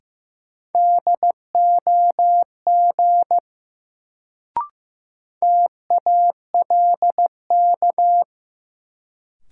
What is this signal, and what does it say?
Morse code, heard as a keyed tone, about 15 wpm, 700 Hz. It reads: DOG TALK
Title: Sentences from Top 400 Words - New Word Drill - 15wpm